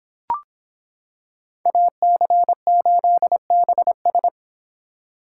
Morse code, heard as a keyed tone, about 26 wpm, 700 Hz. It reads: AC8BS